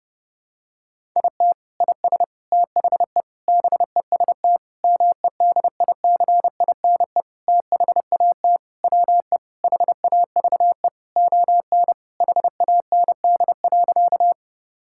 Morse code, heard as a keyed tone, about 30 wpm, 700 Hz. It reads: IT IS THE BEST MEDICINE THAT WE HAVE ON HAND.
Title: Sentences from Top 1000 Words - Review Part 6 - 30wpm